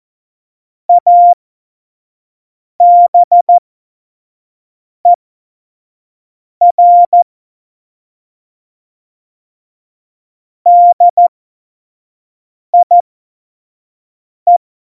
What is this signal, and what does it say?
Morse code, heard as a keyed tone, about 14 wpm, spaced as for 5 wpm, 700 Hz. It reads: ABER DIE
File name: Morsekurs / Witze / Witz 01. September 2020, 25 F70